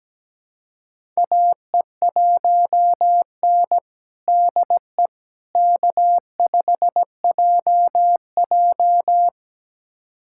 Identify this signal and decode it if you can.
Morse code, heard as a keyed tone, about 17 wpm, 700 Hz. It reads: AE1N DE K5JJ
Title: USA Callsign DE Callsign - 4 Characters - 17wpm